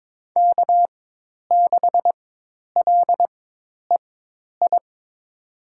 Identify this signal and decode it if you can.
Morse code, heard as a keyed tone, about 22 wpm, spaced as for 10 wpm, 700 Hz. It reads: K6LEI
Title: USA Call Signs - 5 Characters - 4x Character Spacing - 22wpm